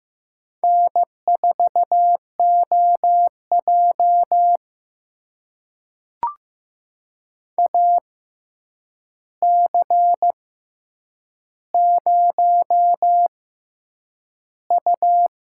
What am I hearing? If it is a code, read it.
Morse code, heard as a keyed tone, about 15 wpm, spaced as for 8 wpm, 700 Hz. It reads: N4OJ AC0U
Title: USA Call Signs - 4 Characters - 6x Character Spacing - 15wpm